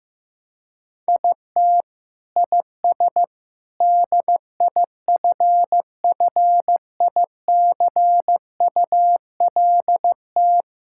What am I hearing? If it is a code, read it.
Morse code, heard as a keyed tone, about 15 wpm, 700 Hz. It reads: IT IS DIFFICULT
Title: Sentences from Top 1000 Words - Review Part 5 - 15wpm